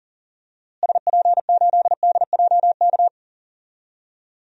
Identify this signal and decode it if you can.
Morse code, heard as a keyed tone, about 40 wpm, 700 Hz. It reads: SP8DJK
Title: DXCC-100 Call Signs - 6 Characters - Repeated 3x - 40wpm